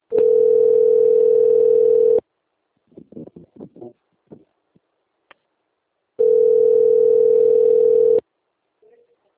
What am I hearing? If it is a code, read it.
Ringback tone (North America), 440 + 480 Hz.